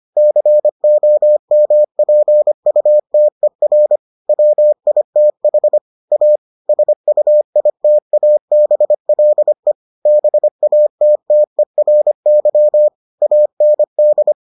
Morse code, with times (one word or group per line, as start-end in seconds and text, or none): 0.2-4.0 COMPUTER
4.3-5.8 WITH
6.1-6.4 A
6.7-9.7 SUITABLE
10.1-12.9 BATTERY
13.2-14.3 AND